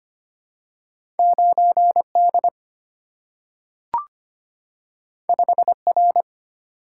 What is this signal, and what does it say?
Morse code, heard as a keyed tone, about 25 wpm, 700 Hz. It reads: 9D 5R
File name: Two Letter-Number Combinations - 25wpm